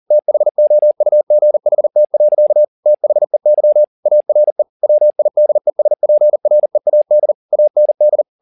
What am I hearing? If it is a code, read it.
Morse code, heard as a keyed tone, about 40 wpm, 600 Hz. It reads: THOUGHT. THEY ARE WIDESPREAD AND